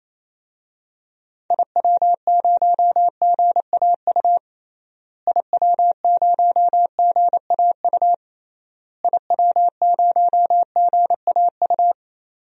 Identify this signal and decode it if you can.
Morse code, heard as a keyed tone, about 28 wpm, 700 Hz. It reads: IW0GAU IW0GAU IW0GAU
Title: DXCC-100 Call Signs - 6 Characters - Repeated 3x - 28wpm